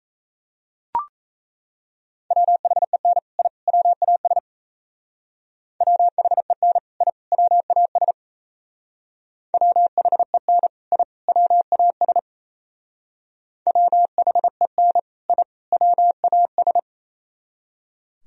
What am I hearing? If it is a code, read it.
Morse code, heard as a keyed tone, about 35 wpm, 700 Hz. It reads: WHEN I WAS WHEN I WAS WHEN I WAS WHEN I WAS
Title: Speed-Racing - Sets of 3 Words 28wpm